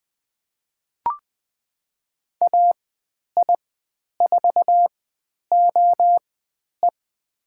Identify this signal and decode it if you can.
Morse code, heard as a keyed tone, about 20 wpm, spaced as for 10 wpm, 700 Hz. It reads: AI4OE
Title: 109 Lesson 18g - Letter m - Call Signs 20wpm